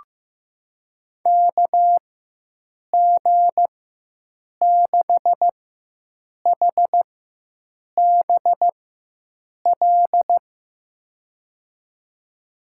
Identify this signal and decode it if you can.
Morse code, heard as a keyed tone, about 15 wpm, spaced as for 7 wpm, 700 Hz. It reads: KG6HBL
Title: USA Call Signs - 6 Characters - 4x Character Spacing - 15wpm